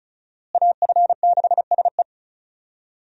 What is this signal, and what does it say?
Morse code, heard as a keyed tone, about 36 wpm, 700 Hz. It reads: AF6SE
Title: USA Call Signs - 5 Characters - Repeated 3x - 35wpm